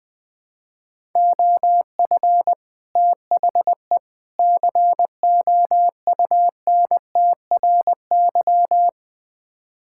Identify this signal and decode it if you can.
Morse code, heard as a keyed tone, about 21 wpm, 700 Hz. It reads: OF THE COUNTRY
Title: Build A Phrase - 3-Word Phrases - 20wpm